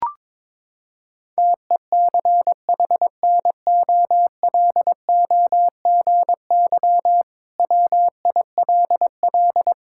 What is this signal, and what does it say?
Morse code, heard as a keyed tone, about 22 wpm, 700 Hz. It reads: TECHNOLOGY WILL
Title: Sentences from Top 1000 Words - Review Part 5 - 22wpm